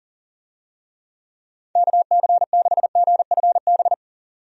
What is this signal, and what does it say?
Morse code, heard as a keyed tone, about 40 wpm, 700 Hz. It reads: KC6GFB
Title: USA Call Signs - 6 Characters - 40wpm